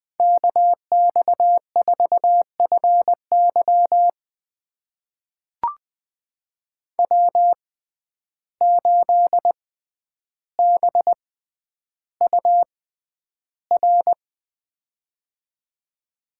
Morse code, heard as a keyed tone, about 20 wpm, spaced as for 10 wpm, 700 Hz. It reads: KX4FY W8BUR
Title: USA Call Signs - 5 Characters - 6x Character Spacing - 20wpm